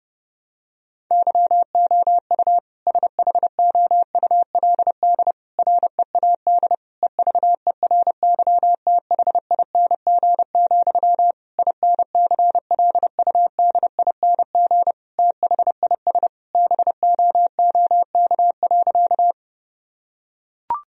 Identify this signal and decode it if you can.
Morse code, heard as a keyed tone, about 30 wpm, 700 Hz. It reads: YOU SHOULD READ EVERYTHING, INCLUDING THIS BOOK.
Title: Sentences from Top 1000 Words - Review Part 4 - 30wpm